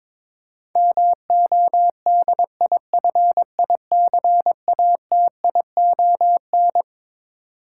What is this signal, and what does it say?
Morse code, heard as a keyed tone, about 22 wpm, 700 Hz. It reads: MODIFICATION
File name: Twelve Letter Words - 22wpm